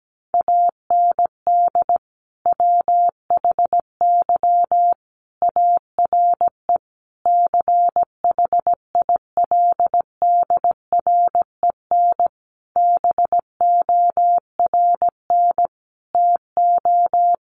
Morse code, heard as a keyed tone, about 17 wpm, 700 Hz. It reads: AND WHY ARE CHILDREN BORN TO
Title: Sentences from Top 1000 Words 17wpm